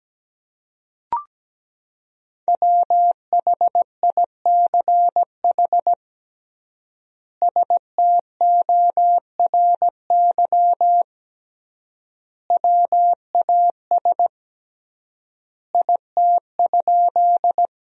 Morse code, heard as a keyed tone, about 17 wpm, 700 Hz. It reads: WHICH STORY WAS IT?
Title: 3x Word-Spacing - Sentences from Top 300 Words - 17wpm